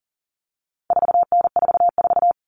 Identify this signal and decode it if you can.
Morse code, heard as a keyed tone, about 40 wpm, 700 Hz. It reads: 4N44
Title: All Cut Numbers - 4-Digit 40wpm